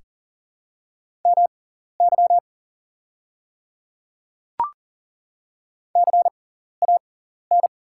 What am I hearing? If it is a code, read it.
Morse code, heard as a keyed tone, about 40 wpm, spaced as for 14 wpm, 700 Hz. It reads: MY CAN
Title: 6x Character Spacing - Top 100 Words - 40wpm